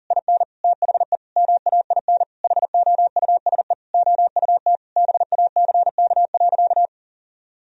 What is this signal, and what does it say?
Morse code, heard as a keyed tone, about 40 wpm, 700 Hz. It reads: IN THE MAIN HOUSE OUT BACK.
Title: Sentences from Top 1000 Words - Review Part 2 - 40wpm